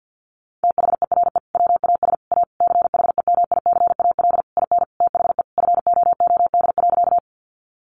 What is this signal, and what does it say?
Morse code, heard as a keyed tone, about 50 wpm, 700 Hz. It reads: THERE WAS A CHEMICAL IN THE FOOD.
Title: Sentences from Top 1200 Words 50wpm